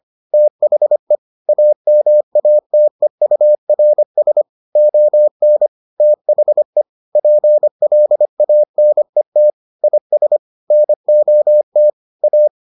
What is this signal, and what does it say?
Morse code, heard as a keyed tone, about 25 wpm, 600 Hz. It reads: THE AMATEURS ON THE PLANET IS NOT A